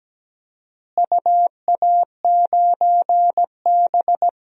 Morse code, heard as a keyed tone, about 17 wpm, 700 Hz. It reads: UA9B
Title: DXCC-100 Call Signs - 4 Characters - Repeated 2x - 17wpm